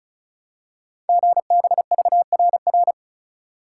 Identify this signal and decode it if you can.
Morse code, heard as a keyed tone, about 36 wpm, 690 Hz. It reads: GBVRR
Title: Code Groups - 5 Letters - 35wpm